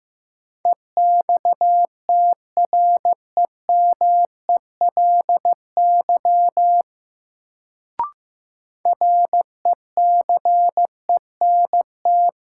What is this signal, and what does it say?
Morse code, heard as a keyed tone, about 15 wpm, 700 Hz. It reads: EXTREMELY RECENT
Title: Sentences from Top 1000 Words - Review Part 7 - New Word Drill - 15wpm